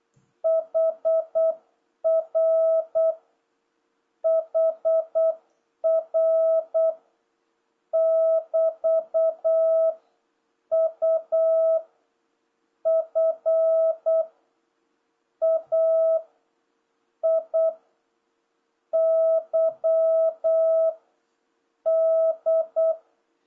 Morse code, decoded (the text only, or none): HR HR <BT> U F A I Y D